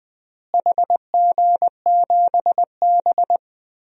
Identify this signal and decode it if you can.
Morse code, heard as a keyed tone, about 19 wpm, 700 Hz. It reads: HG7B
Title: DXCC-100 Call Signs - 4 Characters - Repeated 2x - 20wpm